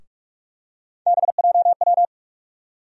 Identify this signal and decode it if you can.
Morse code, heard as a keyed tone, about 46 wpm, 700 Hz. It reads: BJW
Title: USA Call Sign Postfixes - 3 Letters 45wpm